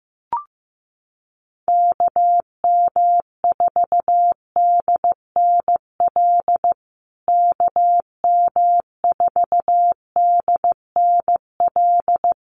Morse code, heard as a keyed tone, about 15 wpm, 700 Hz. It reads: KM4DNL KM4DNL
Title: USA Call Signs Repeated 3x 15wpm